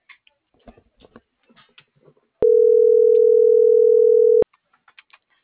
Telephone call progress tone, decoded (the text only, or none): ringback tone